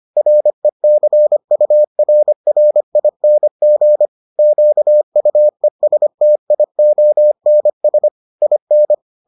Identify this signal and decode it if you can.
Morse code, heard as a keyed tone, about 25 wpm, 600 Hz. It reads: RECURRING QUESTIONS IN